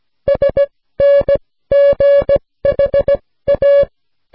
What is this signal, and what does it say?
Morse code, heard as a keyed tone, about 16 wpm, spaced as for 14 wpm, 560 Hz. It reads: SNGHA
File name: snelheid 3